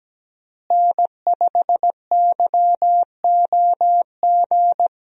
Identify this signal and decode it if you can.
Morse code, heard as a keyed tone, about 17 wpm, 700 Hz. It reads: N5YOG
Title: USA Call Signs - 5 Characters - Repeated 2x - 17wpm